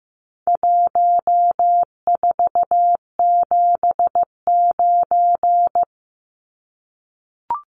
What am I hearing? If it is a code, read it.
Morse code, heard as a keyed tone, about 15 wpm, 700 Hz. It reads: 1479
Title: Numbers - Contest Style - 4-Digit 15wpm